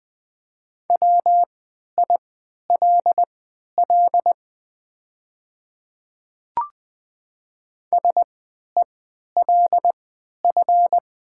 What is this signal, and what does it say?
Morse code, heard as a keyed tone, about 20 wpm, spaced as for 11 wpm, 700 Hz. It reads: WILL SELF
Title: 3x Character Spacing - Top 200 Words - 20wpm